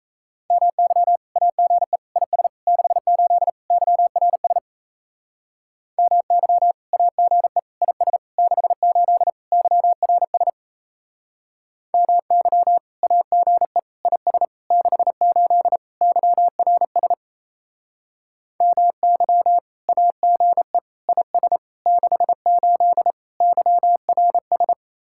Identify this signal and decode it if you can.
Morse code, heard as a keyed tone, about 35 wpm, 700 Hz. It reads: MY AGE IS 68 YRS MY AGE IS 68 YRS MY AGE IS 68 YRS MY AGE IS 68 YRS